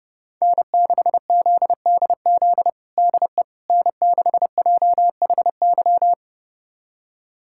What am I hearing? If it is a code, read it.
Morse code, heard as a keyed tone, about 30 wpm, 700 Hz. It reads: N6ZDZ DE N6JHY
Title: USA Callsign DE Callsign - 5 Characters - 30wpm